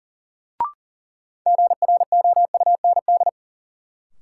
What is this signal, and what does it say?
Morse code, heard as a keyed tone, about 40 wpm, 700 Hz. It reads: GROUND
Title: Sets of 3 Words - New Word Drill - 40wpm